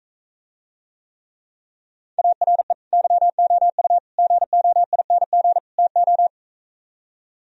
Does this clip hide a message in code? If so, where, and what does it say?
Yes: Morse code from 2.2 to 6.3 s, heard as a keyed tone, about 45 wpm, 700 Hz. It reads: ARE YOU GOING TO